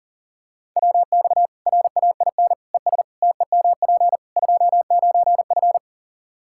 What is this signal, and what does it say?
Morse code, heard as a keyed tone, about 40 wpm, 700 Hz. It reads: WX RAIN ES TEMP 29F
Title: Practice QSO - Exchange 2 - Info - 40wpm